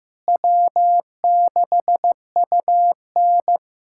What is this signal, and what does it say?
Morse code, heard as a keyed tone, about 15 wpm, 700 Hz. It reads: W6UN
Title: USA Callsign DE Callsign - 4 Characters - 15wpm